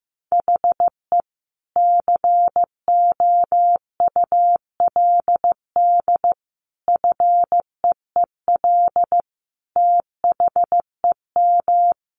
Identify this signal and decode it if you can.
Morse code, heard as a keyed tone, about 15 wpm, 700 Hz. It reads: HE COULD FEEL THEM
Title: A Sound Of Thunder 15wpm